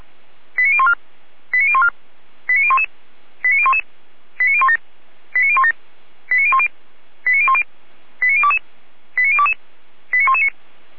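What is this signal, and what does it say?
Five-tone selective call (ZVEI), heard as quick runs of five tones, tones of about 70 ms. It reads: 89014 89014 89011 89011 89017 89017 89010 89010 89022 89022 89119
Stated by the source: Received on SDR as a ZVEI five-tone selective call, code 89060.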